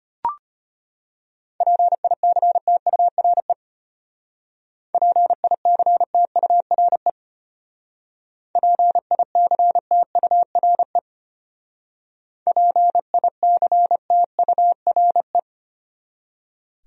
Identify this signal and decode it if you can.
Morse code, heard as a keyed tone, about 31 wpm, 700 Hz. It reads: PICTURE PICTURE PICTURE PICTURE
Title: Speed-Racing Top 500 Words 25wpm